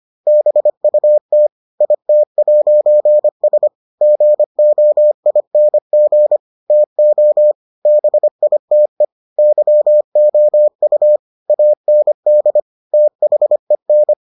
Morse code, heard as a keyed tone, about 25 wpm, 600 Hz. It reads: BUT IT'S GOING TO BITE YOU AND THEN